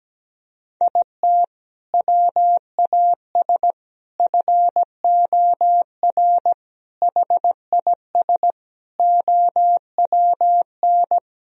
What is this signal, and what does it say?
Morse code, heard as a keyed tone, about 17 wpm, 700 Hz. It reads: IT WAS FOR HIS OWN
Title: Sentences from Top 1000 Words - Review Part 3 - 17wpm